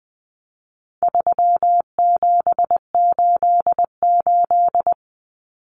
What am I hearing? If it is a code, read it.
Morse code, heard as a keyed tone, about 20 wpm, 700 Hz. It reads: 3788
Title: Common Cut Numbers - 4-Digit 20wpm